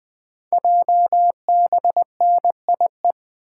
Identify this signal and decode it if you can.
Morse code, heard as a keyed tone, about 20 wpm, 700 Hz. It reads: JBNIE
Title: Code Groups - 5 Letters - 20wpm